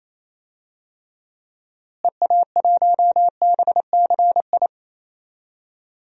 Morse code, heard as a keyed tone, about 28 wpm, 700 Hz. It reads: EA1BCI
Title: DXCC-100 Call Signs - 6 Characters - Repeated 3x - 28wpm